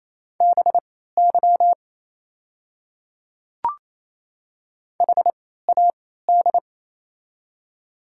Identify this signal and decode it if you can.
Morse code, heard as a keyed tone, about 28 wpm, spaced as for 16 wpm, 700 Hz. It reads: BY HAD
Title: 3x Character Spacing - Top 100 Words - 28wpm